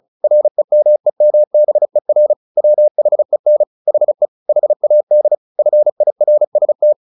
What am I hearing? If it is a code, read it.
Morse code, heard as a keyed tone, about 34 wpm, 600 Hz. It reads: REMEMBER WHEN HE HAD FIRST
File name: the old man and the sea / the old man and the sea 1 35wpm